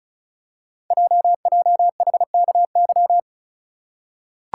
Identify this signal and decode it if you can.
Morse code, heard as a keyed tone, about 36 wpm, 690 Hz. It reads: JJHKY
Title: Code Groups - 5 Letters - 35wpm